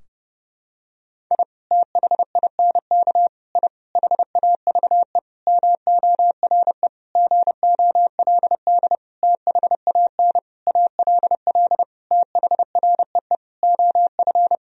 Morse code, heard as a keyed tone, about 32 wpm, 700 Hz. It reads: I THINK I HAVE MORE GOLD THAN ALL THREE OF
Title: Sentences from Top 1000 Words - Review Part 3 - 30wpm